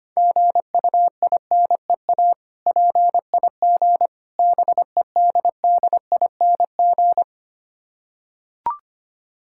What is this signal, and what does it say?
Morse code, heard as a keyed tone, about 25 wpm, 700 Hz. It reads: GUINEA PIG BEDDING
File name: practice / 25wpm / Difficult Words with Hints - 25wpm